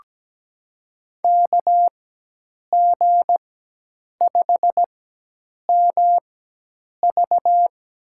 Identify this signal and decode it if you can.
Morse code, heard as a keyed tone, about 17 wpm, spaced as for 8 wpm, 700 Hz. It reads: KG5MV
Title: USA Call Signs - 4x Character Spacing - 17wpm